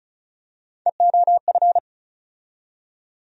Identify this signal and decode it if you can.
Morse code, heard as a keyed tone, about 34 wpm, 700 Hz. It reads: EOF
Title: USA Call Sign Postfixes - 3 Letters 35wpm